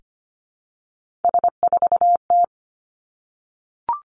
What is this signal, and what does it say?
Morse code, heard as a keyed tone, about 25 wpm, 700 Hz. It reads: S4T
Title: Three Challenging Letter-Number Combinations 25wpm